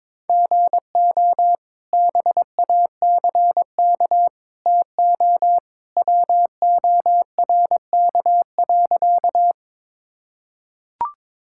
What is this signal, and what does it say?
Morse code, heard as a keyed tone, about 22 wpm, 700 Hz. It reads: GO BACK TO WORK.
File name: practice / 22wpm / Sentences with 2-Word Phrases - 22wpm